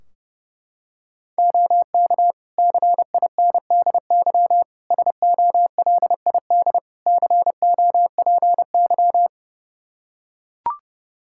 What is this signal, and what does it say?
Morse code, heard as a keyed tone, about 30 wpm, 700 Hz. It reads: OK CINDY SOLID COPY
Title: K1USN SST QSO Practice 30wpm